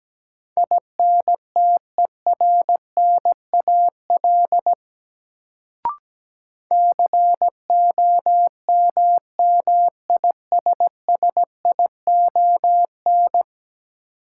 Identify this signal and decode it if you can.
Morse code, heard as a keyed tone, about 17 wpm, 700 Hz. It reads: INTERNAL COMMISSION
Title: Sentences from Top 1400 Words - New Word Drill - 17wpm